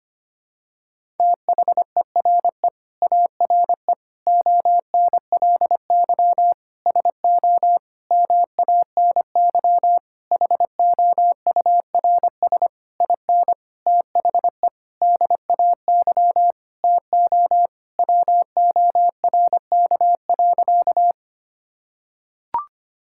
Morse code, heard as a keyed tone, about 25 wpm, 700 Hz. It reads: THERE ARE ONLY SO MANY HOURS IN THE DAY TO WORK.
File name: practice / 25wpm / Sentences with 9-Word Phrases - 25wpm